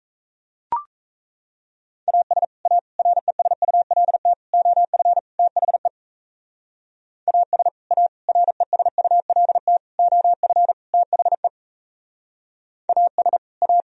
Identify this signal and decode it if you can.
Morse code, heard as a keyed tone, about 35 wpm, 700 Hz. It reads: AS A RESULT OF THE AS A RESULT OF THE AS A